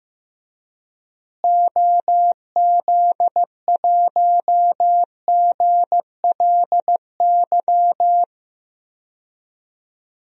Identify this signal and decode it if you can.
Morse code, heard as a keyed tone, about 15 wpm, 700 Hz. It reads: OZ1GLY
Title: DXCC-100 Call Signs - 6 Characters - Repeated 2x - 15wpm